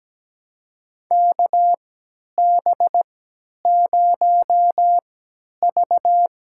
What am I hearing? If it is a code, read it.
Morse code, heard as a keyed tone, about 17 wpm, spaced as for 10 wpm, 700 Hz. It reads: KB0V
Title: USA Call Signs - 3x Character Spacing - 17wpm